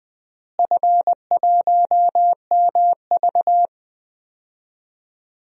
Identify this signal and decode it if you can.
Morse code, heard as a keyed tone, about 21 wpm, 700 Hz. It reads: F1MV
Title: DXCC-100 Call Signs - 4 Characters - Repeated 2x - 20wpm